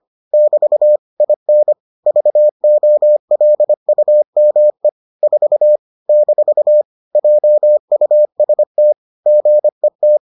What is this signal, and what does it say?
Morse code, heard as a keyed tone, about 25 wpm, 600 Hz. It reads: <BT> IN VOLUME 4 - JUST GET